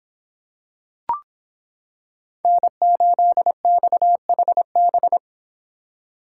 Morse code, heard as a keyed tone, about 26 wpm, 700 Hz. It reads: N8XHB